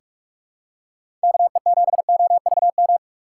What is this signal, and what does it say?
Morse code, heard as a keyed tone, about 46 wpm, 700 Hz. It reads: KE7OVM
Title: USA Call Signs - 6 Characters - 6x Character Spacing - 45wpm